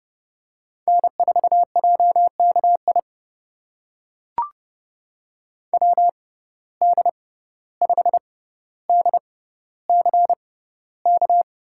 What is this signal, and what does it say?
Morse code, heard as a keyed tone, about 30 wpm, spaced as for 15 wpm, 700 Hz. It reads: N4JKI WD5DCK
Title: USA Call Signs - 6x Character Spacing - 30wpm